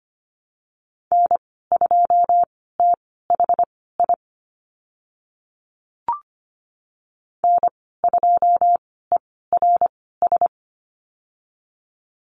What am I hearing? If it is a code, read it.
Morse code, heard as a keyed tone, about 25 wpm, spaced as for 16 wpm, 690 Hz. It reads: N2THI N2ERS
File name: Lesson 15g - Number 5 - Call Signs 25wpm